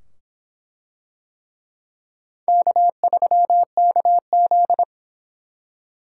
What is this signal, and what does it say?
Morse code, heard as a keyed tone, about 24 wpm, 700 Hz. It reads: K3KZ